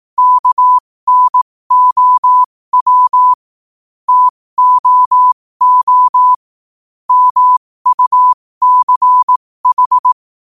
Morse code, heard as a keyed tone, about 18 wpm, spaced as for 15 wpm, 1000 Hz. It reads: KNOW TOO MUCH
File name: qotdfiles / 15wpm